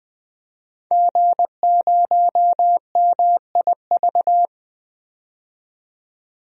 Morse code, heard as a keyed tone, about 20 wpm, 700 Hz. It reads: G0MIV
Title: DXCC-100 Call Signs - 5 Characters - 20wpm